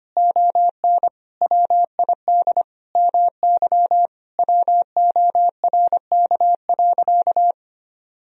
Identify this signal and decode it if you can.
Morse code, heard as a keyed tone, about 25 wpm, 700 Hz. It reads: ON WID MY WORK.
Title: Sentences from Top 300 Words and Abbreviations 25wpm